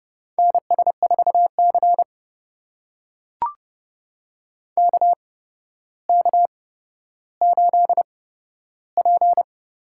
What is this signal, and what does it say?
Morse code, heard as a keyed tone, about 30 wpm, spaced as for 13 wpm, 700 Hz. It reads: NS4C KK8P
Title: USA Call Signs - 4 Characters - 8x Character Spacing - 30wpm